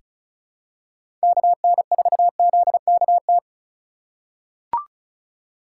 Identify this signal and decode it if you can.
Morse code, heard as a keyed tone, about 34 wpm, 700 Hz. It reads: KN4ZKT
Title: K1USN SST QSO Practice - 35wpm